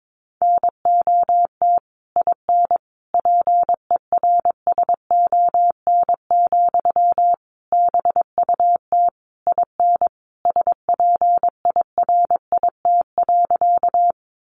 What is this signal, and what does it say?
Morse code, heard as a keyed tone, about 22 wpm, 700 Hz. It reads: NOT IN PERSON, BUT IN SPIRIT.
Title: Sentences from Top 1400 Words 22wpm